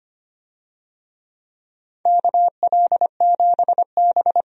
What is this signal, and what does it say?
Morse code, heard as a keyed tone, about 25 wpm, 700 Hz. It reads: KL7B
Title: DXCC-100 Call Signs - 4 Characters - 25wpm